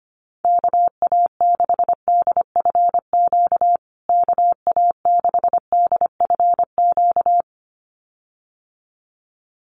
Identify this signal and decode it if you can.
Morse code, heard as a keyed tone, about 25 wpm, 700 Hz. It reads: KA6DFQ KA6DFQ
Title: USA Call Signs Repeated 2x 25wpm